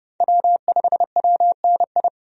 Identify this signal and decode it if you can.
Morse code, heard as a keyed tone, about 29 wpm, 700 Hz. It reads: W5WNI
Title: USA Call Signs - 5 Characters - 30wpm